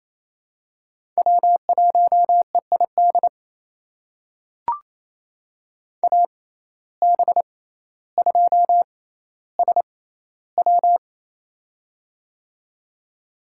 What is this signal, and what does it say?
Morse code, heard as a keyed tone, about 28 wpm, spaced as for 14 wpm, 700 Hz. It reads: W1EID AB2SW
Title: USA Call Signs - 6x Character Spacing - 28wpm